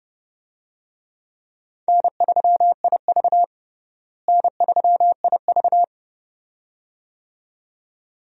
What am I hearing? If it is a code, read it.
Morse code, heard as a keyed tone, about 30 wpm, 700 Hz. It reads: N3IV N3IV
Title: USA Call Signs - 4 Characters - Repeated 2x - 30wpm